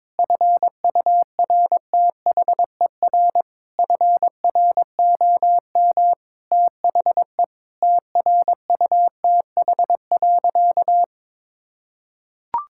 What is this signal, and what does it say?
Morse code, heard as a keyed tone, about 22 wpm, 700 Hz. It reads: FURTHER FROM THE TRUTH.
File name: practice / 22wpm / Sentences with 8-Word Phrases - 22wpm